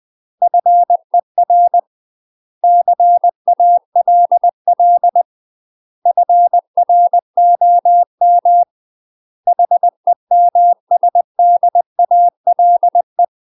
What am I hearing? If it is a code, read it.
Morse code, heard as a keyed tone, about 20 wpm, 700 Hz. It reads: FER CALL FROM HEMSDALE
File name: UK MASTER QSOs TRAINING / 20WPM / UK MASTER QSOs 0058